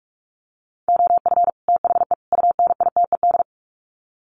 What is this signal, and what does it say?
Morse code, heard as a keyed tone, about 44 wpm, 700 Hz. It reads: OF THE UNITED
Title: Sets of 3 Words 45wpm